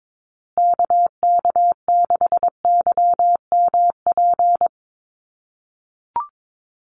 Morse code, heard as a keyed tone, about 22 wpm, 700 Hz. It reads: KK6YMP
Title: USA Call Signs Repeated 3x 22wpm